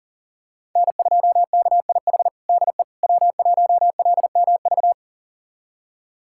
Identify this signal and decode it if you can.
Morse code, heard as a keyed tone, about 40 wpm, 700 Hz. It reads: N2KIH DE W1LMV
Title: USA Callsign DE Callsign - 5 Characters - 40wpm